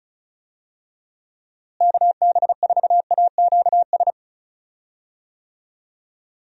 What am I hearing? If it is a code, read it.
Morse code, heard as a keyed tone, about 34 wpm, 700 Hz. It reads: KB4AQS
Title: USA Call Signs - 6 Characters - Repeated 2x - 35wpm